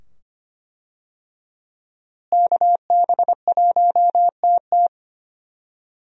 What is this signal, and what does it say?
Morse code, heard as a keyed tone, about 25 wpm, 690 Hz. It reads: KB1TT
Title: USA Call Signs - 5 Characters - Repeated 2x - 25wpm